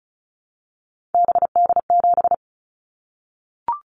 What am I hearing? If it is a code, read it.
Morse code, heard as a keyed tone, about 36 wpm, 700 Hz. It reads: BD7
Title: Callsign Prefixes for Top 100 DXCC - 35wpm